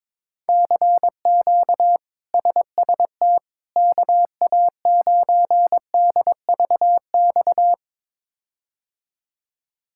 Morse code, heard as a keyed tone, about 21 wpm, 700 Hz. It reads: CQ SST KA9DVX
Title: K1USN SST QSO Practice - 22wpm